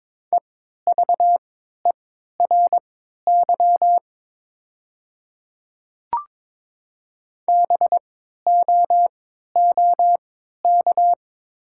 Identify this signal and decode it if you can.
Morse code, heard as a keyed tone, about 22 wpm, spaced as for 12 wpm, 700 Hz. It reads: EVERY BOOK
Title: 3x Character Spacing - Top 300 Words - 22wpm